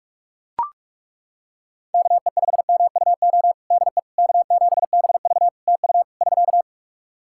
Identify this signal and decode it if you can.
Morse code, heard as a keyed tone, about 41 wpm, 700 Hz. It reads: KE5MUO DE K7BV TU <SK>
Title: K1USN SST QSO Practice 45wpm